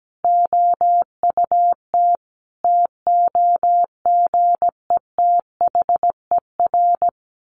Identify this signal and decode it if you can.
Morse code, heard as a keyed tone, about 17 wpm, 700 Hz. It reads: OUT TOGETHER
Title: Sentences from Top 600 Words 17wpm